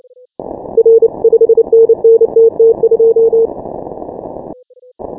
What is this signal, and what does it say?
Morse code, heard as a keyed tone, about 29 wpm, 450 Hz. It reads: R 5NN TT2
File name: final / W1UJ-168-140-MorseRunner